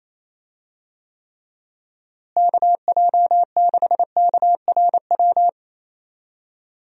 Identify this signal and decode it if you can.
Morse code, heard as a keyed tone, about 28 wpm, 700 Hz. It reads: KJ6KRW
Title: USA Call Signs - 6 Characters - Repeated 3x - 28wpm